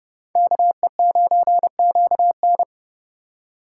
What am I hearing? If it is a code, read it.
Morse code, heard as a keyed tone, about 30 wpm, 700 Hz. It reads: KE9QN